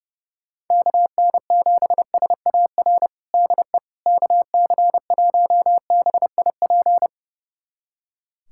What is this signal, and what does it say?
Morse code, heard as a keyed tone, about 30 wpm, 700 Hz. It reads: KN7SAR DE KC1BIP
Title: USA Callsign DE Callsign 30wpm